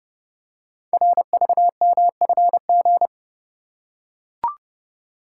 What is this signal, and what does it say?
Morse code, heard as a keyed tone, about 29 wpm, 700 Hz. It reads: RVMFG
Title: Code Groups - 5 Letter:Numbers - 30wpm